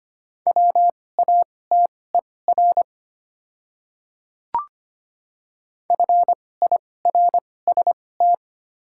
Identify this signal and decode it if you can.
Morse code, heard as a keyed tone, about 25 wpm, spaced as for 18 wpm, 700 Hz. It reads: WATER FIRST
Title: 2x Character Spacing - Top 300 Words - 25wpm